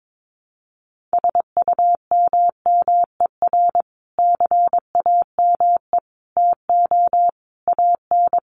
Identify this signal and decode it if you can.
Morse code, heard as a keyed tone, about 22 wpm, 700 Hz. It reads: SUMMER CAME TO AN
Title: Sentences from Top 1100 Words 22wpm